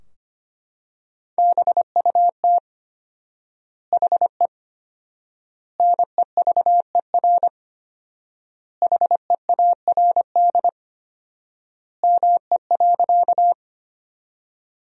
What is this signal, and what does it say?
Morse code, heard as a keyed tone, about 25 wpm, 700 Hz. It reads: BUT HE NEVER HEARD ME.
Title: 4x Word-Spacing - Sentences from Top 400 Words - 25wpm